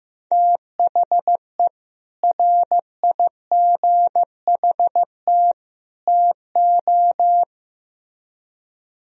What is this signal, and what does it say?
Morse code, heard as a keyed tone, about 16 wpm, 700 Hz. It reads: THE RIGHT TO
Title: Speed-Racing - Sets of 3 Words 15wpm